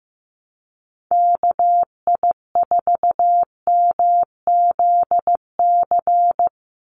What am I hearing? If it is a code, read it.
Morse code, heard as a keyed tone, about 15 wpm, 700 Hz. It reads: KI4MZC
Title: Call Signs Repeated 2x - 15wpm